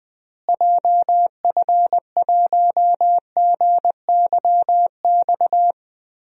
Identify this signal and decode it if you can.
Morse code, heard as a keyed tone, about 20 wpm, 700 Hz. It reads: JF1GYX
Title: DXCC-100 Call Signs - 6 Characters - Repeated 3x - 20wpm